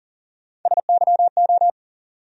Morse code, heard as a keyed tone, about 40 wpm, 700 Hz. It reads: SYO
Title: USA Call Sign Postfixes - 3 Letters 40wpm